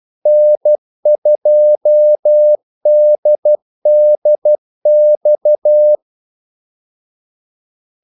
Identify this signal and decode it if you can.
Morse code, heard as a keyed tone, about 12 wpm, 600 Hz. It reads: N2DDX
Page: 2 ekstra ordmellemrum mellem kaldesignalerne